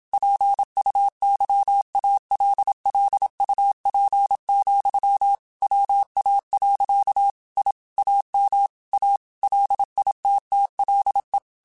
Morse code, heard as a keyed tone, about 27 wpm, 800 Hz. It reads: PUYALLUP, WA. I AM A LITTLE